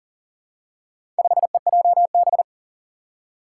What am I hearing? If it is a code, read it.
Morse code, heard as a keyed tone, about 43 wpm, 690 Hz. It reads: 5EJB
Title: Four Challenging Letter-Number Combinations 40wpm